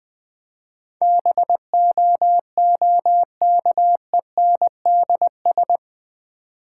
Morse code, heard as a keyed tone, about 20 wpm, 700 Hz. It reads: BOOKENDS
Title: Easy Words with Hints - 20wpm